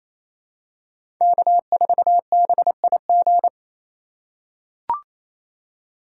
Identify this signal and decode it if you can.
Morse code, heard as a keyed tone, about 28 wpm, 700 Hz. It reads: K4BIG
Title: USA Call Signs - 5 Characters - 4x Character Spacing - 28wpm